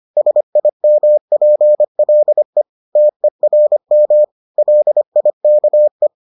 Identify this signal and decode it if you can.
Morse code, heard as a keyed tone, about 25 wpm, 600 Hz. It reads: SIMPLE TERM LIKE